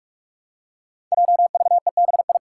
Morse code, heard as a keyed tone, about 44 wpm, 700 Hz. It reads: JVEBI
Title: Five Challenging Letter-Number Combinations 45wpm